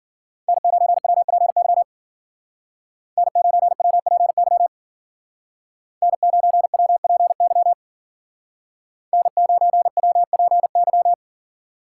Morse code, heard as a keyed tone, about 48 wpm, 700 Hz. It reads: N9WPY N9WPY N9WPY N9WPY
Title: Speed Racing - USA Call Signs 40wpm